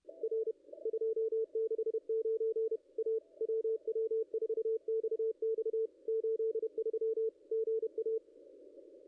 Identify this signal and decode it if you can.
Morse code, heard as a keyed tone, about 31 wpm, 440 Hz, with radio noise. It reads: R 269 A WW4XX 83 GA
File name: contests / ss-cw-k5tr-op-n5rz -0200